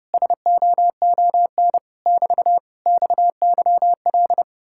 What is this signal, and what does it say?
Morse code, heard as a keyed tone, about 33 wpm, 700 Hz. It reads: SOON <BT> XYL